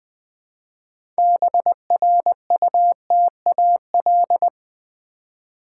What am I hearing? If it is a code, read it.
Morse code, heard as a keyed tone, about 20 wpm, 700 Hz. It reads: BRUTAL
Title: 6 Letter Words - First 3 Letters Given - 20wpm